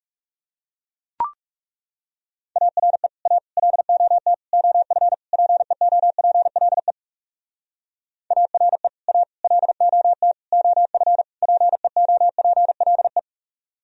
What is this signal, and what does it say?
Morse code, heard as a keyed tone, about 35 wpm, 700 Hz. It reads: ARE A LOT OF PEOPLE ARE A LOT OF PEOPLE
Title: Speed-Racing - Sets of 5 Words 30wpm